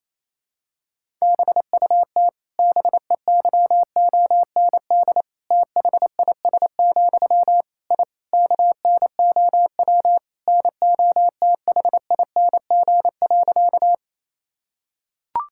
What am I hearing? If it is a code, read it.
Morse code, heard as a keyed tone, about 28 wpm, 700 Hz. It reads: BUT BEYOND THIS, I KNOW NOTHING.
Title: Sentences from Top 1000 Words - Review Part 6 - 28wpm